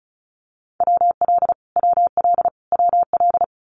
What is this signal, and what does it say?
Morse code, heard as a keyed tone, about 34 wpm, 700 Hz. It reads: WL WL WL
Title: Abbreviations Repeated 3x - 35wpm